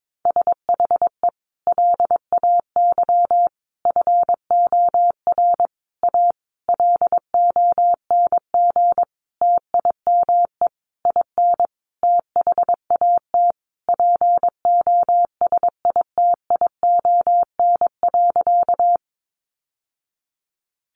Morse code, heard as a keyed tone, about 22 wpm, 700 Hz. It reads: SHE LAY FOR A LONG TIME IN THAT POSITION.
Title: Sentences from Top 900 Words 22wpm